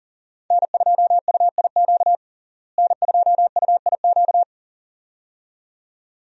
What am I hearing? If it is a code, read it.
Morse code, heard as a keyed tone, about 40 wpm, 700 Hz. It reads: N2UIQ N2UIQ
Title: USA Call Signs - 5 Characters - Repeated 2x - 40wpm